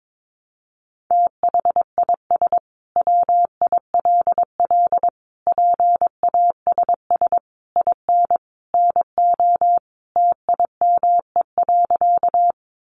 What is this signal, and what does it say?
Morse code, heard as a keyed tone, about 22 wpm, 700 Hz. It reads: THIS WILL PASS IN NO TIME.
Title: Sentences from Top 400 Words 22wpm